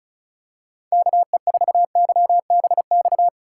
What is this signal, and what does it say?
Morse code, heard as a keyed tone, about 36 wpm, 700 Hz. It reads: KE4YBX
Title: USA Call Signs - 6 Characters - 3x Character Spacing - 35wpm